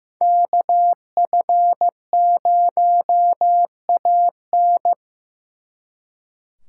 Morse code, heard as a keyed tone, about 17 wpm, 700 Hz. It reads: KF0AN
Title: Speed Racing - USA 2x2 Call Signs 15wpm